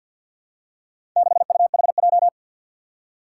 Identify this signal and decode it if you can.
Morse code, heard as a keyed tone, about 52 wpm, 700 Hz. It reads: 6UHJ
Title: Four Challenging Letter-Number Combinations 50wpm